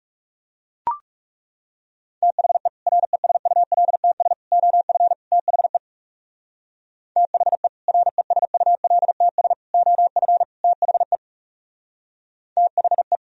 Morse code, heard as a keyed tone, about 37 wpm, 700 Hz. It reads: THE RESULTS OF THE THE RESULTS OF THE THE
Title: Speed-Racing - Sets of 4 Words 30wpm